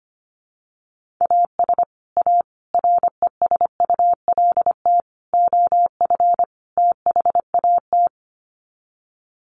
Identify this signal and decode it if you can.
Morse code, heard as a keyed tone, about 25 wpm, 700 Hz. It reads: AS A RESULT OF THAT
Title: Sets of 5 Words Encore 25wpm